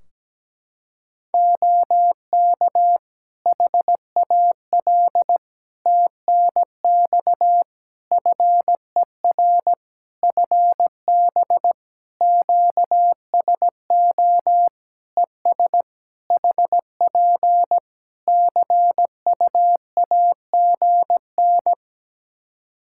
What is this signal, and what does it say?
Morse code, heard as a keyed tone, about 17 wpm, 700 Hz. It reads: OK HAL TNX FER FB QSO ES HP CUAGN
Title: Practice QSO - Exchange 4 - Ending - 17wpm